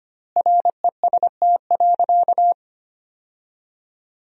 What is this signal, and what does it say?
Morse code, heard as a keyed tone, about 25 wpm, 700 Hz. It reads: REST.
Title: 4x Word-Spacing - Sentences from Top 500 Words - 25wpm